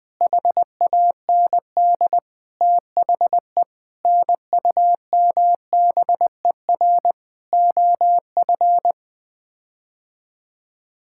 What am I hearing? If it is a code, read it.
Morse code, heard as a keyed tone, about 21 wpm, 700 Hz. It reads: HAND THE NUMBER OF
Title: Sets of 8 Words - 20wpm